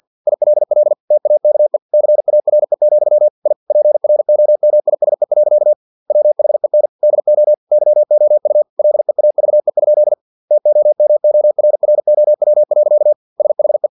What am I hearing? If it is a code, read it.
Morse code, heard as a keyed tone, about 50 wpm, 600 Hz. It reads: ILL TAKE CARE, I PROMISE. WHEN DO YOU LEAVE? TOMORROW. SHE
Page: Never 49wpm